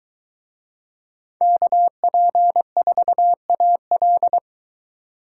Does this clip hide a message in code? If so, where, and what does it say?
Yes: Morse code from 1.4 to 4.4 s, heard as a keyed tone, about 23 wpm, 700 Hz. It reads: KP4AL